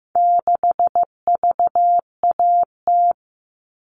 Morse code, heard as a keyed tone, about 15 wpm, 700 Hz. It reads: 6VAT
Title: All Cut Numbers - 4-Digit 15wpm